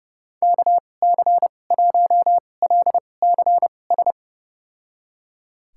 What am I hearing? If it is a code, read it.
Morse code, heard as a keyed tone, about 30 wpm, spaced as for 22 wpm, 700 Hz. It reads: KC1LCS
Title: USA Call Signs - 6 Characters - 2x Character Spacing - 30wpm